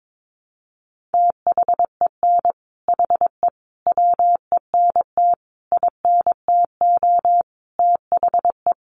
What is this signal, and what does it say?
Morse code, heard as a keyed tone, about 22 wpm, 700 Hz. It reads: THEN HE WENT INTO THE
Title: Sentences from Top 1500 Words 22wpm